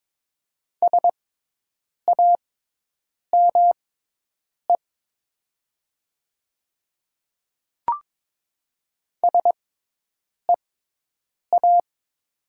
Morse code, heard as a keyed tone, about 22 wpm, spaced as for 8 wpm, 700 Hz. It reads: SAME SEA
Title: 6x Character Spacing - Top 300 Words - 22wpm